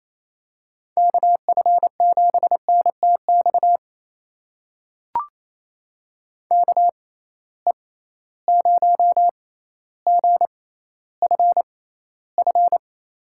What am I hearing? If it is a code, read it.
Morse code, heard as a keyed tone, about 28 wpm, spaced as for 14 wpm, 700 Hz. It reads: KF7NTX KE0GFF
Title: USA Call Signs - 6 Characters - 6x Character Spacing - 28wpm